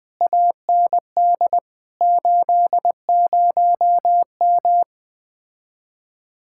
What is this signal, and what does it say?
Morse code, heard as a keyed tone, about 20 wpm, 700 Hz. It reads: AND 80M
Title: Speed-Racing Common QSO Phrases Encore - 17wpm